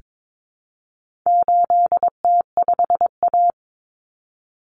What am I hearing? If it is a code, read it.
Morse code, heard as a keyed tone, about 22 wpm, 700 Hz. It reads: 8T5A
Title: Common Cut Numbers - 4-Digit 22wpm